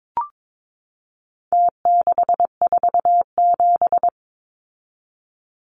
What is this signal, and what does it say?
Morse code, heard as a keyed tone, about 22 wpm, 700 Hz. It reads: T647
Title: Numbers - Contest Style - 4-Digit 22wpm